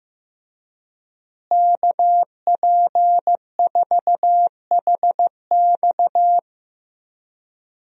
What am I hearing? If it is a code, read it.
Morse code, heard as a keyed tone, about 16 wpm, 700 Hz. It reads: KP4HX
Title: USA Call Signs - 5 Characters - Repeated 3x - 15wpm